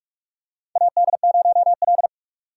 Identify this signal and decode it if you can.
Morse code, heard as a keyed tone, about 47 wpm, 700 Hz. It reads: AD0L
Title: USA Call Signs - 4 Characters - 6x Character Spacing - 45wpm